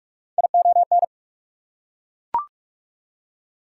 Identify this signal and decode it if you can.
Morse code, heard as a keyed tone, about 42 wpm, 700 Hz. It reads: ION
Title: Prefixes and Suffixes 45wpm